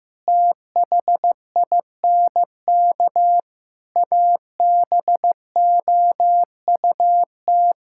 Morse code, heard as a keyed tone, about 15 wpm, 700 Hz. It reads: THINK ABOUT
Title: Sentences from Top 1600 Words 15wpm